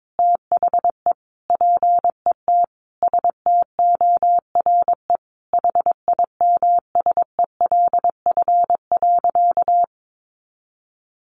Morse code, heard as a keyed tone, about 22 wpm, 700 Hz. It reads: THE PET STORE HIMSELF.
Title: Additional Easy Phrases 22wpm